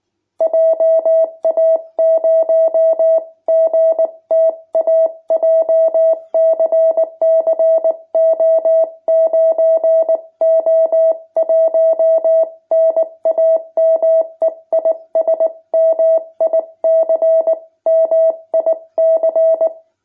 Morse code, heard as a keyed tone, about 19 wpm, 650 Hz. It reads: JA0GTA JCC O9O1 NAME IS MIC MIC
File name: jh8cbh / 2021cw